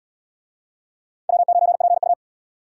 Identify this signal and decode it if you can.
Morse code, heard as a keyed tone, about 77 wpm, 700 Hz. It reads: W9PU